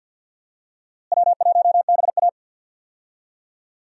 Morse code, heard as a keyed tone, about 52 wpm, 700 Hz. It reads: W1BA
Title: USA Call Signs - 4 Characters - Repeated 2x - 50wpm